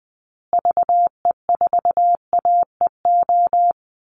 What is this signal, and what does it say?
Morse code, heard as a keyed tone, about 20 wpm, 700 Hz. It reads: VE4AEO
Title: Call Signs Repeated 2x - 20wpm